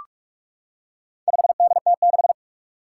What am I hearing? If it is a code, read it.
Morse code, heard as a keyed tone, about 46 wpm, 700 Hz. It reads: 5DT6
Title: Four Challenging Letter-Number Combinations 45wpm